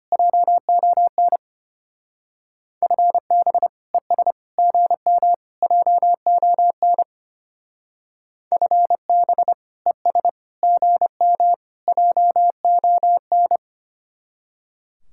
Morse code, heard as a keyed tone, about 32 wpm, 700 Hz. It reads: JON FB ES GM JON FB ES GM JON